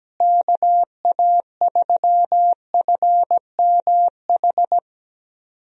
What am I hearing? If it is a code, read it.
Morse code, heard as a keyed tone, about 17 wpm, 700 Hz. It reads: KA3FMH
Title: USA Call Signs - 6 Characters - Repeated 2x - 17wpm